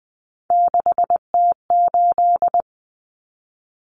Morse code, heard as a keyed tone, about 20 wpm, 700 Hz. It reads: 6T8
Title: Common Cut Numbers - 3-Digit 20wpm